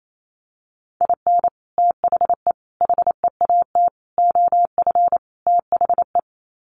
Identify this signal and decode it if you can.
Morse code, heard as a keyed tone, about 28 wpm, 700 Hz. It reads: IN THE HEAT OF THE
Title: Sets of 5 Words Encore2 28wpm